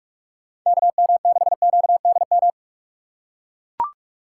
Morse code, heard as a keyed tone, about 43 wpm, 700 Hz. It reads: KM6QDM
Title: USA Call Signs - 6 Characters - 4x Character Spacing - 45wpm